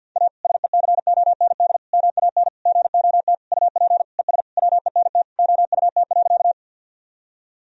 Morse code, heard as a keyed tone, about 49 wpm, 700 Hz. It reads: A SECOND MAN GOT UP ES WENT OUT.
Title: Sentences from Top 300 Words and Abbreviations 50wpm